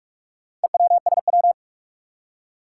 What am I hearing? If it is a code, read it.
Morse code, heard as a keyed tone, about 44 wpm, 700 Hz. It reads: EWSW
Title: Four Challenging Letter-Number Combinations 45wpm